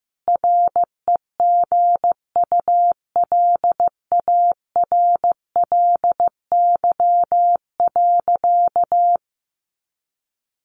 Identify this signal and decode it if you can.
Morse code, heard as a keyed tone, about 15 wpm, 700 Hz. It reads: REGULARLY.
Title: Sentences from Top 1400 Words 15wpm